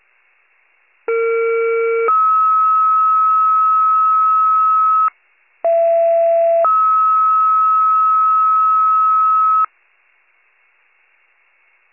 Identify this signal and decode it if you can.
Two-tone page, heard as a one-second tone, then a three-second tone, about 460 then 1290 Hz, about 670 then 1290 Hz. It is heard from 1.1 to 9.7 s.
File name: pager / 41 RESCUE DUTY / wcalert